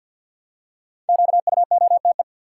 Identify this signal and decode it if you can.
Morse code, heard as a keyed tone, about 49 wpm, 700 Hz. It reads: QUOTE
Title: Five Letter Words - 50wpm